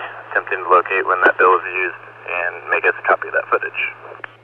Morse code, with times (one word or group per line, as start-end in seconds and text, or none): none